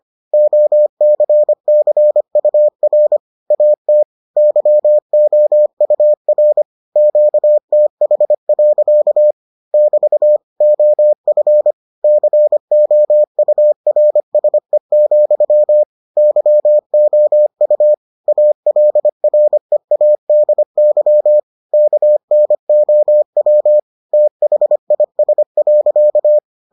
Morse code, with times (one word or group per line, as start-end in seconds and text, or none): 0.3-3.2 OCCUR
3.5-4.0 AT
4.4-6.6 YOUR
7.0-9.3 QTH.
9.7-10.4 <BT>
10.6-11.7 OF
12.0-15.8 COURSE,
16.2-17.9 YOU
18.3-21.4 ALREADY
21.7-23.8 KNOW
24.1-26.4 THIS.